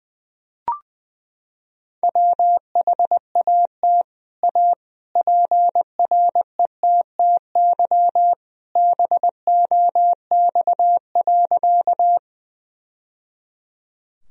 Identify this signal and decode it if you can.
Morse code, heard as a keyed tone, about 20 wpm, 700 Hz. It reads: WHAT A PRETTY BOX.
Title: Sentences from Top 1000 Words - Review Part 2 - 20wpm